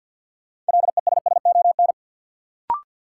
Morse code, heard as a keyed tone, about 50 wpm, 700 Hz. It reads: LESSON
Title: Sentences from Top 1200 Words - New Word Drill - 50wpm